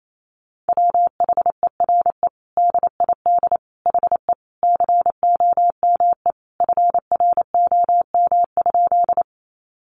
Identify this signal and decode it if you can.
Morse code, heard as a keyed tone, about 28 wpm, 700 Hz. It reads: WHERE DID HE COME FROM?
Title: Sentences from Top 200 Words 28wpm